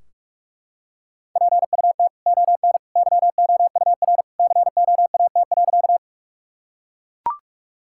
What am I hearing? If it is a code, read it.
Morse code, heard as a keyed tone, about 42 wpm, 700 Hz. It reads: PUT ON YOUR COAT.
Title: Sentences from Top 1600 Words 45wpm